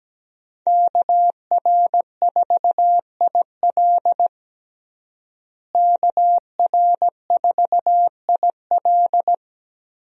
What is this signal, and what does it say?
Morse code, heard as a keyed tone, about 17 wpm, 700 Hz. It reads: KR4IL KR4IL
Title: USA Call Signs - 5 Characters - Repeated 2x - 17wpm